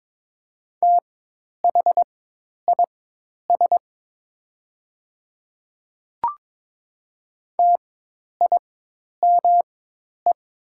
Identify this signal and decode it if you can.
Morse code, heard as a keyed tone, about 22 wpm, spaced as for 10 wpm, 700 Hz. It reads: THIS TIME
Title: 4x Character Spacing - Top 100 Words - 22wpm